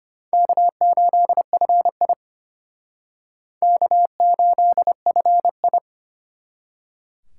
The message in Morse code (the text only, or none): K8FI K8FI